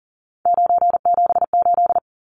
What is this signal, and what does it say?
Morse code, heard as a keyed tone, about 40 wpm, 700 Hz. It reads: 978
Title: Numbers - 3-Digit 40wpm